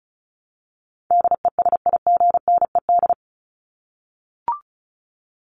Morse code, heard as a keyed tone, about 34 wpm, 700 Hz. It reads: DESIGNED
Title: ED Suffix - Top 100 Words with Base Words 35wpm